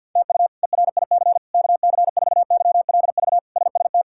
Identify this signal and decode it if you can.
Morse code, heard as a keyed tone, about 51 wpm, 700 Hz. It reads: TU ERIC KC4YLV SST
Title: K1USN SST QSO Practice - 50wpm